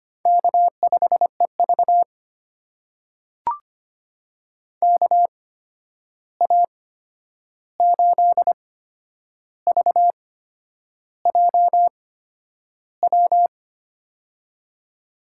Morse code, heard as a keyed tone, about 25 wpm, spaced as for 9 wpm, 700 Hz. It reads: K5EV KA8VJW
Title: USA Call Signs - 8x Character Spacing - 25wpm